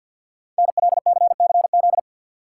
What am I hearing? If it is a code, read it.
Morse code, heard as a keyed tone, about 51 wpm, 700 Hz. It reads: NLCCZ
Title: Code Groups - 5 Letters - 50wpm